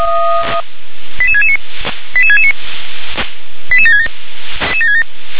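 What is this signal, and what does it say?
Five-tone selective call (ZVEI), heard as quick runs of five tones, tones of about 70 ms. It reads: 88669 88669 88767
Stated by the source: ZVEI five-tone call received by SDR.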